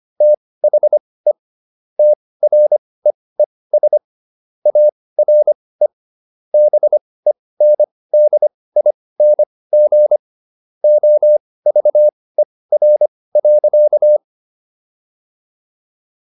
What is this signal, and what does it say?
Morse code, heard as a keyed tone, about 25 wpm, spaced as for 18 wpm, 600 Hz. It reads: THE TREES ARE BENDING OVER.